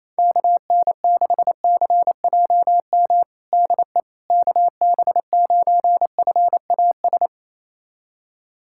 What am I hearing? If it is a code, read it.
Morse code, heard as a keyed tone, about 28 wpm, 700 Hz. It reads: KN6CJM DE KB9FAS
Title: USA Callsign DE Callsign 28wpm